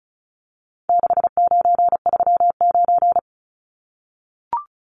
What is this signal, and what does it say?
Morse code, heard as a keyed tone, about 34 wpm, 700 Hz. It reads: 6939
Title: Numbers - 4-Digit 35wpm